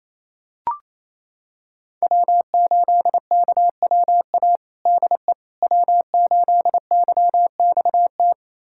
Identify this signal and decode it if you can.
Morse code, heard as a keyed tone, about 27 wpm, 700 Hz. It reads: W8KWA DE W8YXT
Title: USA Callsign DE Callsign - 5 Characters - 28wpm